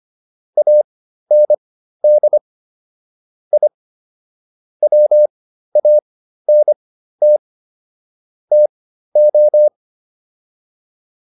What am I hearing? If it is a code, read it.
Morse code, heard as a keyed tone, about 25 wpm, spaced as for 13 wpm, 600 Hz. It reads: AND I WANT TO